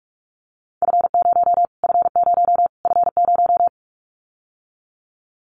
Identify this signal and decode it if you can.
Morse code, heard as a keyed tone, about 46 wpm, 700 Hz. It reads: F0 F0 F0
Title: Callsign Prefixes for Top 25 DXCC - Repeated 3x - 45wpm